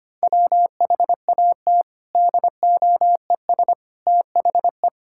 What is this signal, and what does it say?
Morse code, heard as a keyed tone, about 25 wpm, 700 Hz. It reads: WHAT DOES THE
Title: Sentences from Top 1000 Words - Review Part 6 - 25wpm